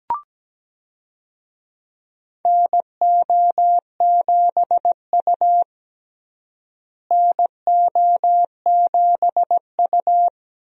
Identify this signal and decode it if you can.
Morse code, heard as a keyed tone, about 17 wpm, 700 Hz. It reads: NO7U NO7U
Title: USA Call Signs - 4 Characters - Repeated 2x - 17wpm